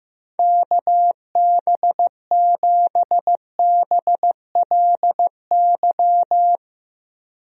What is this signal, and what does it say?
Morse code, heard as a keyed tone, about 15 wpm, 700 Hz. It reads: KB7BLY
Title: USA Call Signs - 6 Characters - Repeated 3x - 15wpm